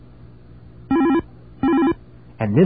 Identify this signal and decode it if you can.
There is no code.